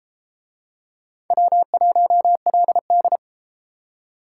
Morse code, heard as a keyed tone, about 31 wpm, 700 Hz. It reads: W1LD